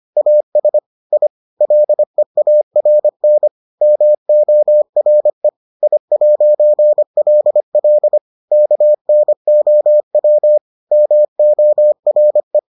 Morse code, heard as a keyed tone, about 25 wpm, 600 Hz. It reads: AS I LEARN MORE I'LL KNOW MORE